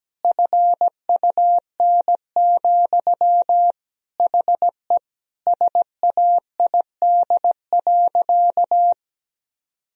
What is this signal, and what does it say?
Morse code, heard as a keyed tone, about 17 wpm, 700 Hz. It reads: FUN, HE SAID.
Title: Sentences from Top 1000 Words - Review Part 4 - 17wpm